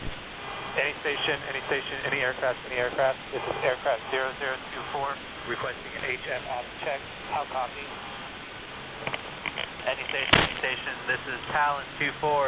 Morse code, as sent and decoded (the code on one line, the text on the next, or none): none